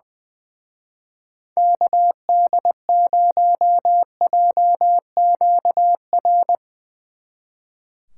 Morse code, heard as a keyed tone, about 22 wpm, 700 Hz. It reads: KD0JQR
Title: Speed Racing - USA 2x3 Call Signs 20wpm